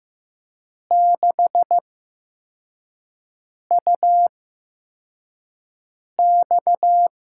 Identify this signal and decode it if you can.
Morse code, heard as a keyed tone, about 15 wpm, spaced as for 4 wpm, 700 Hz. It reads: 6UX
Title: USA Call Signs - 6 Characters - 8x Character Spacing - 15wpm